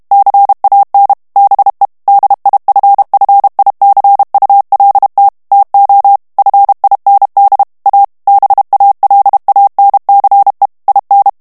Morse code, heard as a keyed tone, about 32 wpm, 800 Hz. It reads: CAN BE DIFFICULT TO FIND A BALANCE IN
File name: morse mp3 / ANGLAIS texte clair 1 vitesse 30 mots minute